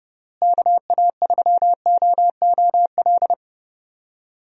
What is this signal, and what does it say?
Morse code, heard as a keyed tone, about 30 wpm, 700 Hz. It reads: KA3OOL
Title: USA Call Signs - 6 Characters - Repeated 3x - 30wpm